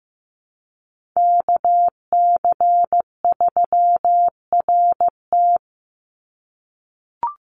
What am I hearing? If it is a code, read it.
Morse code, heard as a keyed tone, about 15 wpm, 700 Hz. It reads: KC3RT
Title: NA SOTA Chaser Callsigns 15wpm